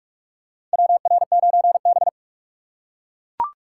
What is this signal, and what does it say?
Morse code, heard as a keyed tone, about 41 wpm, 700 Hz. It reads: WR9B
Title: USA Call Signs - 4 Characters - 6x Character Spacing - 45wpm